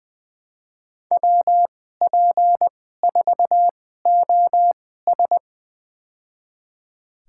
Morse code, heard as a keyed tone, about 20 wpm, spaced as for 15 wpm, 700 Hz. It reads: WP4OS
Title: USA Call Signs - 5 Characters - 2x Character Spacing - 20wpm